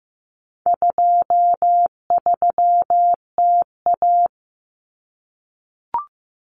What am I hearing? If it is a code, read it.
Morse code, heard as a keyed tone, about 15 wpm, 700 Hz. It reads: 23TA
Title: Common Cut Numbers - 4-Digit 15wpm